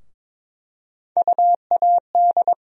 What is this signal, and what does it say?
Morse code, heard as a keyed tone, about 21 wpm, 700 Hz. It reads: UAD
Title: USA Call Sign Postfixes - 3 Letters 22wpm